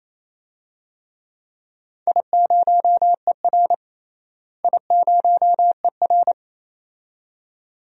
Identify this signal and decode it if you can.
Morse code, heard as a keyed tone, about 28 wpm, 700 Hz. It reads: I0ER I0ER
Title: DXCC-100 Call Signs - 4 Characters - Repeated 2x - 28wpm